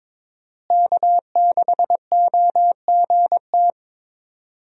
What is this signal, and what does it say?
Morse code, heard as a keyed tone, about 22 wpm, 700 Hz. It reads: K6OGT
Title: USA Call Signs - 5 Characters - Repeated 3x - 22wpm